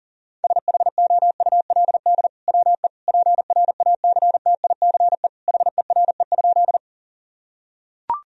Morse code, heard as a keyed tone, about 37 wpm, 700 Hz. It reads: SHOULD WE PRACTICE HERE?
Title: Sentences from Top 1000 Words - Review Part 4 - 40wpm